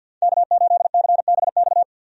Morse code, heard as a keyed tone, about 52 wpm, 700 Hz. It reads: K8CBX
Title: USA Call Signs - 5 Characters - 50wpm